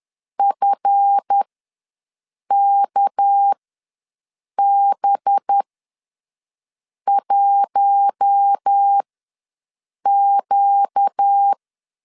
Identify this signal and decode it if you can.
Morse code, heard as a keyed tone, about 10 wpm, spaced as for 5 wpm, 800 Hz. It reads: FKB1Q
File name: LEZIONI / 7 GF9Z / 7 GF9Z ws 10wpm 3919